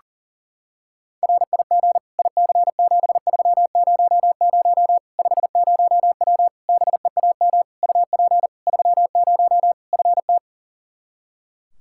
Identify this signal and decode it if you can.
Morse code, heard as a keyed tone, about 40 wpm, 700 Hz. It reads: RIG IC7300 50W BEAM UP 30 FT
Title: Practice QSO - Exchange 2 - Info - 40wpm